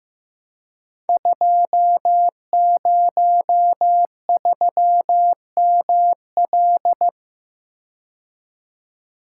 Morse code, heard as a keyed tone, about 15 wpm, 700 Hz. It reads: 203ML
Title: Code Groups - 5 Letter:Numbers - 15wpm